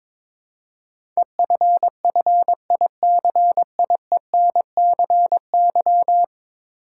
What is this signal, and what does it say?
Morse code, heard as a keyed tone, about 22 wpm, 700 Hz. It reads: EFFICIENCY
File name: practice / 22wpm / 10 Letter Words - First 3 Letters Given - 22wpm